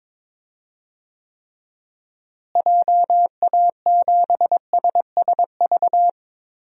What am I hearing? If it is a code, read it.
Morse code, heard as a keyed tone, about 22 wpm, 700 Hz. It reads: JA7SSV
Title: DXCC-100 Call Signs - 6 Characters - Repeated 3x - 22wpm